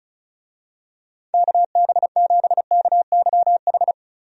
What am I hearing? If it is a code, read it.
Morse code, heard as a keyed tone, about 34 wpm, 700 Hz. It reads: KB7KYH
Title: USA Call Signs - 6 Characters - 35wpm